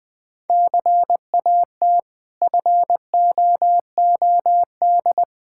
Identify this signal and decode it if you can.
Morse code, heard as a keyed tone, about 20 wpm, 690 Hz. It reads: CAT FOOD
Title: Difficult Words with Hints - 20wpm